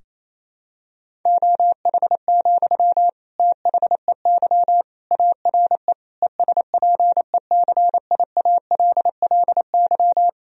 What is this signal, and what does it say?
Morse code, heard as a keyed tone, about 28 wpm, 700 Hz. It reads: OH, THEY ARE ESPECIALLY
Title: Sentences from Top 1000 Words - Review Part 4 - 28wpm